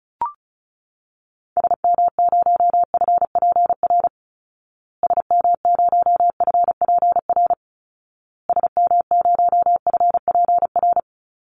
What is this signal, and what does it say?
Morse code, heard as a keyed tone, about 34 wpm, 700 Hz. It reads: SM0FPR SM0FPR SM0FPR
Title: Call Signs Repeated 2x - 35wpm